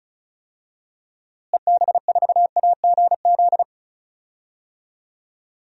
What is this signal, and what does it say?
Morse code, heard as a keyed tone, about 36 wpm, 700 Hz. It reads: EB4AGZ
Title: DXCC-100 Call Signs - 6 Characters - Repeated 3x - 35wpm